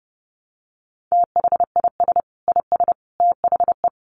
Morse code, heard as a keyed tone, about 30 wpm, 700 Hz. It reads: THIS IS THE
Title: Sentences from Top 1200 Words 30wpm